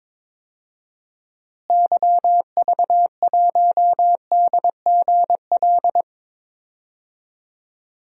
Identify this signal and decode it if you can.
Morse code, heard as a keyed tone, about 22 wpm, 700 Hz. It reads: YV1DGL
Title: DXCC-100 Call Signs - 6 Characters - Repeated 3x - 22wpm